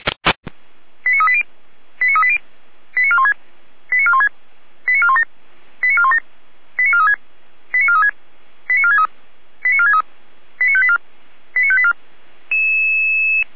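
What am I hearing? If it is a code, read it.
Five-tone selective call (ZVEI), heard as quick runs of five tones, tones of about 70 ms. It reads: 89380 89380 89416 89416 89417 89417 89436 89436 89463 89463 89584 89584
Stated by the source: ZVEI five-tone call received on SDR.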